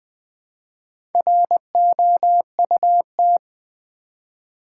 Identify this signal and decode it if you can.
Morse code, heard as a keyed tone, about 20 wpm, 700 Hz. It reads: ROUT
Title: Build A Word - 1 Letter At A Time - 5-Letter Words - 20wpm